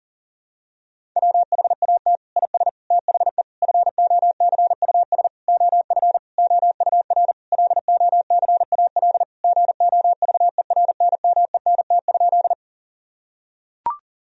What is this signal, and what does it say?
Morse code, heard as a keyed tone, about 40 wpm, 700 Hz. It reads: WHAT IS THE FOCUS OF OUR LOCAL GOVERNMENT?
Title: Sentences from Top 1000 Words - Review Part 4 - 40wpm